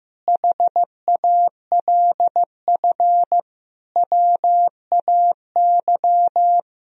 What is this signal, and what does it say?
Morse code, heard as a keyed tone, about 15 wpm, 700 Hz. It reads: HALF WAY
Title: Sentences from Top 1000 Words - Review Part 1 - 15wpm